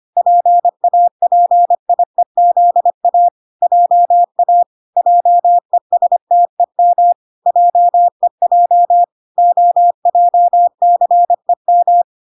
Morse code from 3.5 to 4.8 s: JA